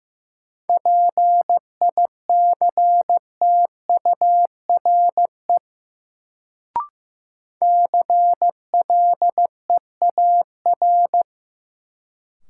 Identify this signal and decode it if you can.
Morse code, heard as a keyed tone, about 15 wpm, 700 Hz. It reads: PICTURE CLEAR
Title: Sentences from Top 500 Words - New Word Drill - 15wpm